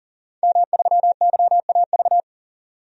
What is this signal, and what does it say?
Morse code, heard as a keyed tone, about 40 wpm, 700 Hz. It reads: M3YAV
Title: DXCC-100 Call Signs - 5 Characters - 40wpm